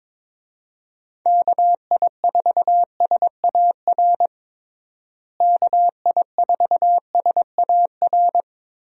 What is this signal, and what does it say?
Morse code, heard as a keyed tone, about 22 wpm, 700 Hz. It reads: KI4SAR KI4SAR
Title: USA Call Signs - 6 Characters - Repeated 3x - 22wpm